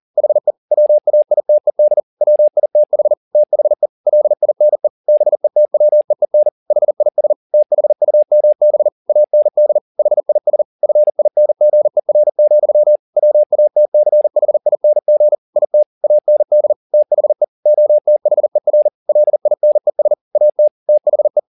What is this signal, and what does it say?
Morse code, heard as a keyed tone, about 40 wpm, 600 Hz. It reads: HE WAITED WITH THE LINE BETWEEN HIS THUMB AND HIS FINGER, WATCHING IT AND THE OTHER LINES AT THE